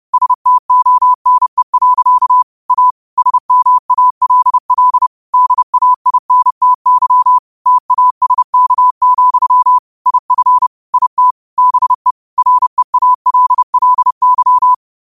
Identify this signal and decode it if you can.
Morse code, heard as a keyed tone, about 30 wpm, 1000 Hz. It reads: STONE. A SMALL DAINTY TASK, IF IT BE REALLY